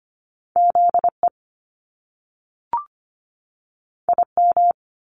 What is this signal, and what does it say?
Morse code, heard as a keyed tone, about 25 wpm, 700 Hz. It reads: ZE IM
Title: Top Two Letter Combinations in Words 25wpm